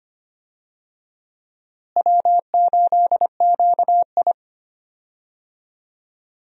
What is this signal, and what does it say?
Morse code, heard as a keyed tone, about 25 wpm, 700 Hz. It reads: W8QI
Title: USA Call Signs - 4 Characters - Repeated 3x - 25wpm